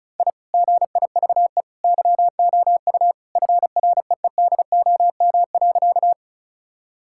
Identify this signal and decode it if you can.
Morse code, heard as a keyed tone, about 36 wpm, 700 Hz. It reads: I GIVE YOU FREEDOM.
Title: Sentences from Top 1000 Words - Review Part 5 - 35wpm